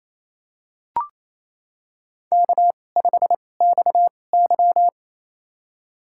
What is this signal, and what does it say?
Morse code, heard as a keyed tone, about 28 wpm, spaced as for 20 wpm, 700 Hz. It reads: K5XY
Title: USA Call Signs - 4 Characters - 2x Character Spacing - 28wpm